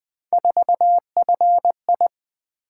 Morse code, heard as a keyed tone, about 20 wpm, 700 Hz. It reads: 4FI
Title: Three Letter-Number Combinations - 20wpm